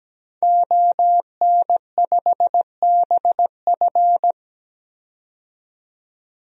Morse code, heard as a keyed tone, about 17 wpm, 700 Hz. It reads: ON5BF
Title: DXCC-100 Call Signs - 5 Characters - 17wpm